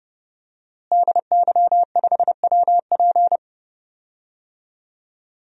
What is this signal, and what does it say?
Morse code, heard as a keyed tone, about 29 wpm, 700 Hz. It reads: DY5WP
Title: DXCC-100 Call Signs - 5 Characters - Repeated 3x - 30wpm